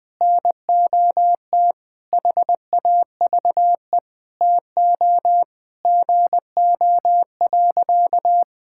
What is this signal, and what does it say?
Morse code, heard as a keyed tone, about 21 wpm, 700 Hz. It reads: NOT HAVE TO GO.
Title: Sentences with 3-Word Phrases - Spoken Beginning - 20wpm